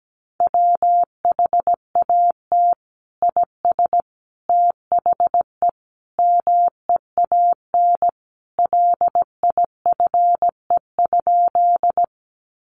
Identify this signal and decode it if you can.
Morse code, heard as a keyed tone, about 17 wpm, 700 Hz. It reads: WHAT IS THE MEAN LIFE?
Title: Sentences from Top 900 Words 17wpm